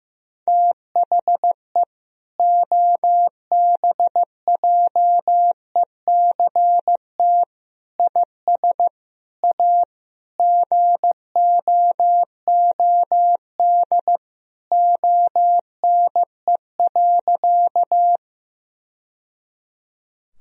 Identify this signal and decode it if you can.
Morse code, heard as a keyed tone, about 15 wpm, 700 Hz. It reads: THE OBJECT IS A GOOD ONE.
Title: Sentences from Top 1000 Words - Review Part 3 - 15wpm